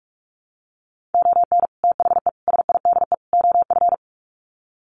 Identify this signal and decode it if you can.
Morse code, heard as a keyed tone, about 44 wpm, 700 Hz. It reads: ON THE SIDE OF
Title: Sets of 4 Words 45wpm